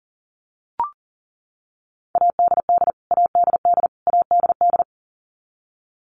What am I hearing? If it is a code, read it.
Morse code, heard as a keyed tone, about 40 wpm, 700 Hz. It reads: ADD ADD ADD
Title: Top 200 Words Repeated 40wpm